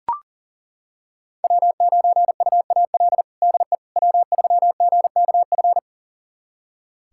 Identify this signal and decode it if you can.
Morse code, heard as a keyed tone, about 40 wpm, 700 Hz. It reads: W9UAL DE W3GKF
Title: USA Callsign DE Callsign - 5 Characters - 40wpm